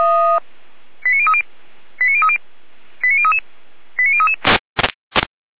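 Five-tone selective call (ZVEI), heard as quick runs of five tones, tones of about 70 ms. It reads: 89030 89030 89033 89033
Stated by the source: ZVEI five-tone call received by SDR.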